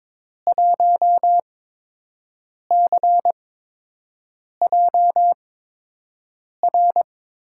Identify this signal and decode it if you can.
Morse code, heard as a keyed tone, about 22 wpm, spaced as for 6 wpm, 700 Hz. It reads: 1CJR
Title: USA Call Signs - 5 Characters - 8x Character Spacing - 22wpm